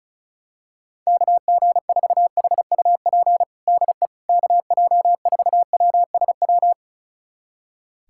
Morse code, heard as a keyed tone, about 36 wpm, 700 Hz. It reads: KG4HUP DE KJ4WSW
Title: USA Callsign DE Callsign - 6 Characters - 35wpm